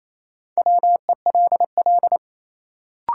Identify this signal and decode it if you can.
Morse code, heard as a keyed tone, about 28 wpm, 700 Hz. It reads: WELL
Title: Sets of 2 Words - New Word Drill - 28wpm